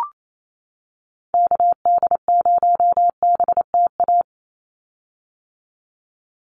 Morse code, heard as a keyed tone, about 28 wpm, 700 Hz. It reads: KD0BTA
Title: USA Call Signs 28wpm